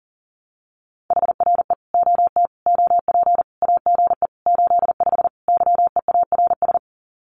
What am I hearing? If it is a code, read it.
Morse code, heard as a keyed tone, about 40 wpm, 700 Hz. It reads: HRE OT OP AGE 85 YEARS
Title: Common QSO Phrases 40wpm